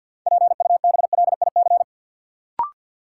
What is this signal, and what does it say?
Morse code, heard as a keyed tone, about 53 wpm, 700 Hz. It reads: PUBLIC
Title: Sentences from Top 600 Words - New Word Drill - 50wpm